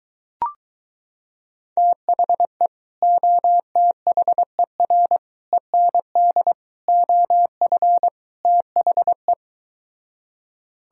Morse code, heard as a keyed tone, about 23 wpm, 700 Hz. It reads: THE OTHER END OF THE